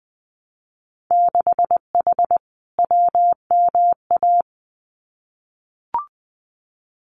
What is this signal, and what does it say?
Morse code, heard as a keyed tone, about 20 wpm, 700 Hz. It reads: 6H WMA
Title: Winter Field Day Exchanges Encore 20wpm